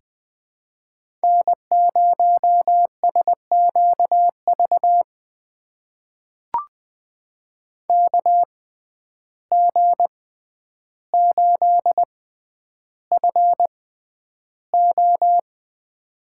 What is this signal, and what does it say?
Morse code, heard as a keyed tone, about 21 wpm, spaced as for 10 wpm, 700 Hz. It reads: N0SQV KG8FO
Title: USA Call Signs - 5 Characters - 6x Character Spacing - 20wpm